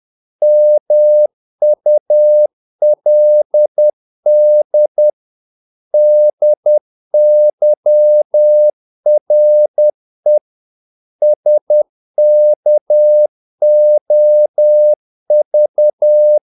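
Morse code, heard as a keyed tone, about 10 wpm, 600 Hz. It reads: MULD DYRE SKOV
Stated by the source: Morse practice of short Danish words at 10 wpm, from CW med Gnister.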